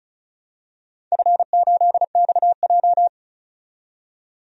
Morse code, heard as a keyed tone, about 36 wpm, 700 Hz. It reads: F8XJ
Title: DXCC-100 Call Signs - 4 Characters - 35wpm